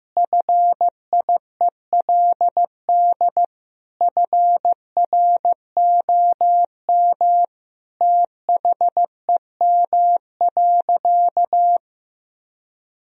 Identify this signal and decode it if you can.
Morse code, heard as a keyed tone, about 16 wpm, 700 Hz. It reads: FIELD FROM THEM.
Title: Sentences from Top 1000 Words - Review Part 2 - 15wpm